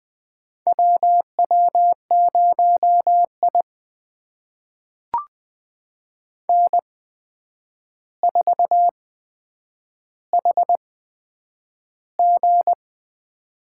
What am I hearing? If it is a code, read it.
Morse code, heard as a keyed tone, about 20 wpm, spaced as for 9 wpm, 700 Hz. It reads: WW0I N4HG
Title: USA Call Signs - 4 Characters - 8x Character Spacing - 20wpm